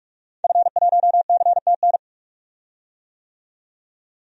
Morse code, heard as a keyed tone, about 46 wpm, 700 Hz. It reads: F1CTN
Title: DXCC-100 Call Signs - 5 Characters - 45wpm